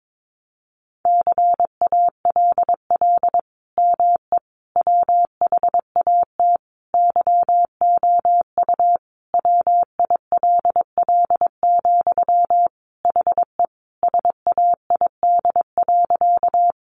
Morse code, heard as a keyed tone, about 22 wpm, 700 Hz. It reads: CALL ME WHAT YOU WILL, HE SAID.
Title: Sentences from Top 200 Words 22wpm